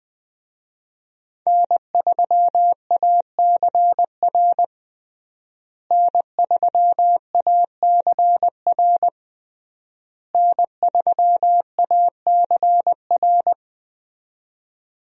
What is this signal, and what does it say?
Morse code, heard as a keyed tone, about 20 wpm, 700 Hz. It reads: N3ACR N3ACR N3ACR
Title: USA Call Signs - 5 Characters - Repeated 3x - 20wpm